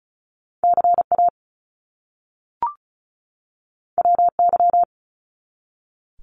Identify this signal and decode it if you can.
Morse code, heard as a keyed tone, about 36 wpm, 700 Hz. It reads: CA WY
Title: US States 35wpm